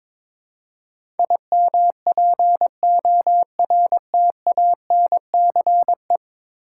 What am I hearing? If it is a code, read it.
Morse code, heard as a keyed tone, about 22 wpm, 700 Hz. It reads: IMPORTANCE
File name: practice / 22wpm / 10 Letter Words - First 3 Letters Given - 22wpm